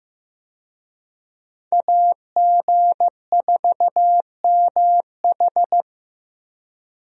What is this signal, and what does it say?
Morse code, heard as a keyed tone, about 16 wpm, 700 Hz. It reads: AG4MH
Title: USA Call Signs - 5 Characters - Repeated 3x - 15wpm